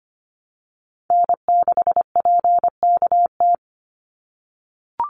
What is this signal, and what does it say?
Morse code, heard as a keyed tone, about 25 wpm, 700 Hz. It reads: N6PKT
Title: NA SOTA Chaser Callsigns 25wpm